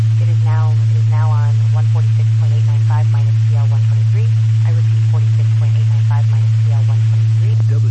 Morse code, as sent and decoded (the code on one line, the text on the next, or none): none